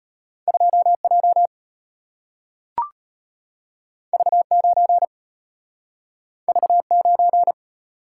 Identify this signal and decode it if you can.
Morse code, heard as a keyed tone, about 34 wpm, 700 Hz. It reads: 2J V9 V9